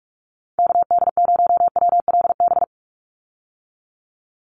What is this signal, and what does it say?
Morse code, heard as a keyed tone, about 46 wpm, 690 Hz. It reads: KD0WLB
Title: USA Call Signs 45wpm